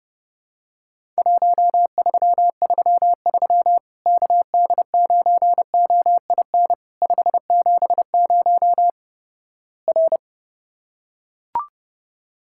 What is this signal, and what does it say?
Morse code, heard as a keyed tone, about 30 wpm, 700 Hz. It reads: 1333 KD9OIN 570 R
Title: Worked All Europe DX Contest - QTC Exchange - 30wpm